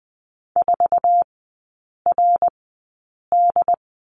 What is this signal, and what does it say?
Morse code, heard as a keyed tone, about 20 wpm, spaced as for 8 wpm, 700 Hz. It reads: 4RD
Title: 138 Lesson 23e - Letter f - Three Characters 20wpm